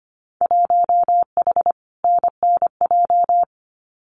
Morse code, heard as a keyed tone, about 25 wpm, 700 Hz. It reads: 1H NNJ
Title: Winter Field Day Exchanges Encore 25wpm